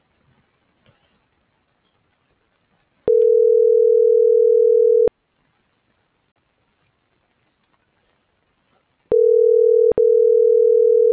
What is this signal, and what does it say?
Ringback tone (North America), 440 + 480 Hz.